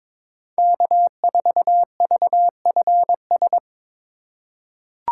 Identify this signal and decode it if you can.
Morse code, heard as a keyed tone, about 22 wpm, 700 Hz. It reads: K4VFS
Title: USA Call Signs - 5 Characters - 3x Character Spacing - 22wpm